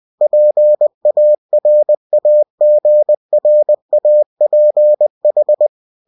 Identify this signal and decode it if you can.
Morse code, heard as a keyed tone, about 20 wpm, 600 Hz. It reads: PARAGRAPH
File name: cw-books / winnie-the-pooh / winnie-20wpm-page-0117